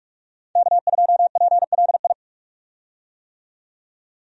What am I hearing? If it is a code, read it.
Morse code, heard as a keyed tone, about 42 wpm, 700 Hz. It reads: K2PLI
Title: USA Call Signs - 5 Characters - 45wpm